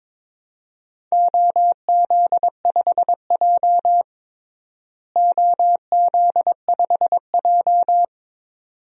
Morse code, heard as a keyed tone, about 22 wpm, 700 Hz. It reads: OZ5J OZ5J
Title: DXCC-100 Call Signs - 4 Characters - Repeated 2x - 22wpm